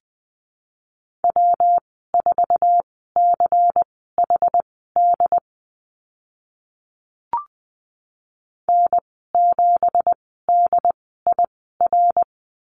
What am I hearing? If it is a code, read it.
Morse code, heard as a keyed tone, about 20 wpm, spaced as for 14 wpm, 700 Hz. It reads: W4CHD N7DIR
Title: 193 Lesson 31g - Letter v - Call Signs 20wpm